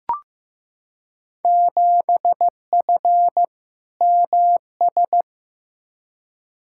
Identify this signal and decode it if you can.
Morse code, heard as a keyed tone, about 15 wpm, 700 Hz. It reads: 7F MS
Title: Field Day Exchanges Encore - 15wpm